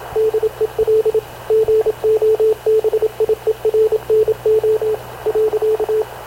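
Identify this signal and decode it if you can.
Morse code, heard as a keyed tone, about 27 wpm, 440 Hz, with radio noise. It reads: DEL GOBIERNO.
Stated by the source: Havana Radio CLA press (PX) broadcast, recorded by Radio Officer Paul Dumesnil aboard mv CNAV QUEST/CZDO.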